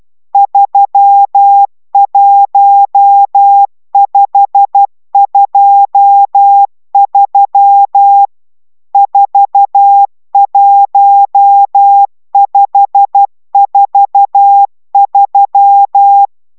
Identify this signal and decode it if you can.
Morse code, heard as a keyed tone, about 12 wpm, 800 Hz. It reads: 31523 41543